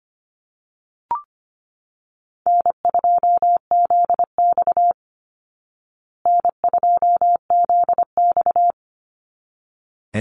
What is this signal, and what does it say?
Morse code, heard as a keyed tone, about 25 wpm, 700 Hz. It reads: N2ZX N2ZX
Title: Call Signs Repeated 3x - 25wpm